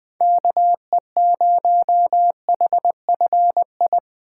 Morse code, heard as a keyed tone, about 23 wpm, 700 Hz. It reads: KE0HFI
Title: Speed Racing - USA Call Signs 20wpm